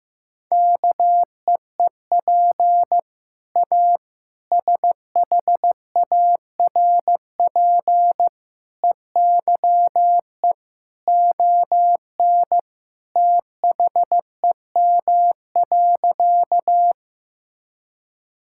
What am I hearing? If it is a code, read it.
Morse code, heard as a keyed tone, about 16 wpm, 700 Hz. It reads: KEEP A SHARP EYE ON THEM.
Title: Sentences from Top 1600 Words 15wpm